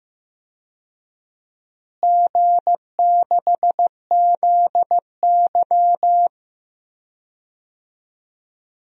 Morse code, heard as a keyed tone, about 15 wpm, 700 Hz. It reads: G6ZY
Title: DXCC-100 Call Signs - 4 Characters - Repeated 2x - 15wpm